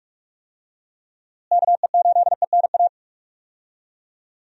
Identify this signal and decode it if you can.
Morse code, heard as a keyed tone, about 44 wpm, 700 Hz. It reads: KE8ENA
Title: USA Call Signs - 6 Characters - Repeated 3x - 45wpm